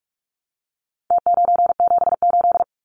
Morse code, heard as a keyed tone, about 44 wpm, 700 Hz. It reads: T978
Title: Numbers - Contest Style - 4-Digit 45wpm